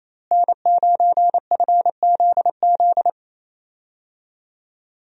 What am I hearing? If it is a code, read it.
Morse code, heard as a keyed tone, about 27 wpm, 700 Hz. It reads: N9FZZ
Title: USA Call Signs - 5 Characters - Repeated 3x - 28wpm